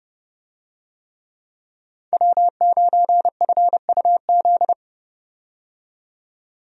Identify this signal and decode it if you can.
Morse code, heard as a keyed tone, about 31 wpm, 700 Hz. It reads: W9FUZ
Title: USA Call Signs - 5 Characters - 30wpm